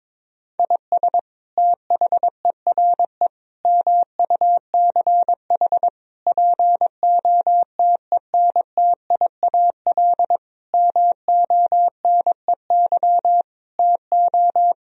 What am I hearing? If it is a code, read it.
Morse code, heard as a keyed tone, about 22 wpm, 700 Hz. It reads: IS THERE MUCH POTENTIAL MONEY TO
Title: Sentences from Top 1000 Words - Review Part 4 - 22wpm